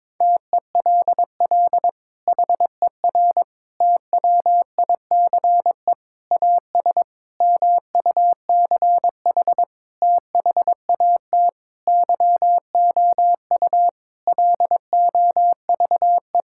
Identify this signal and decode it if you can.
Morse code, heard as a keyed tone, about 22 wpm, 700 Hz. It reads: TELL HER TWICE AS MUCH THAT YOU LOVE
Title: Sentences from Top 1600 Words 22wpm